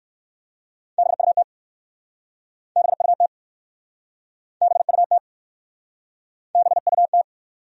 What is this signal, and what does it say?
Morse code, heard as a keyed tone, about 59 wpm, 700 Hz. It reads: BUT BUT BUT BUT